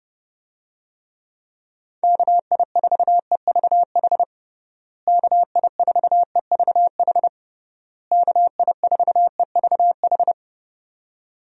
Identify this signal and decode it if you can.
Morse code, heard as a keyed tone, about 30 wpm, 700 Hz. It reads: KI4EVH KI4EVH KI4EVH
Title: USA Call Signs - 6 Characters - Repeated 3x - 30wpm